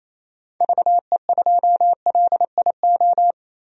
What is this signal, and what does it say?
Morse code, heard as a keyed tone, about 28 wpm, 700 Hz. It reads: VE2LIO
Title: DXCC-100 Call Signs - 6 Characters - Repeated 2x - 28wpm